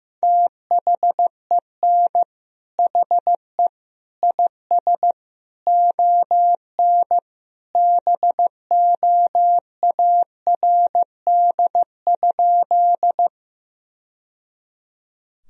Morse code, heard as a keyed tone, about 15 wpm, 700 Hz. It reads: THEN HE IS ON BOARD?
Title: Sentences from Top 1000 Words - Review Part 4 - 15wpm